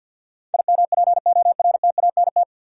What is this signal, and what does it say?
Morse code, heard as a keyed tone, about 53 wpm, 700 Hz. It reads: IMPORTANT
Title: Sets of 3 Words - New Word Drill - 50wpm